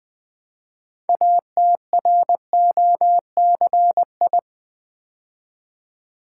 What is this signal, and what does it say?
Morse code, heard as a keyed tone, about 20 wpm, 700 Hz. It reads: ATROCI
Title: Build A Word - 3 Letters At A Time - 8-Letter Words - 20wpm